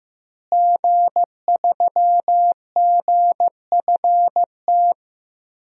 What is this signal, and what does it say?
Morse code, heard as a keyed tone, about 16 wpm, 700 Hz. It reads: G3GFT
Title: DXCC-100 Call Signs - 5 Characters - Repeated 2x - 15wpm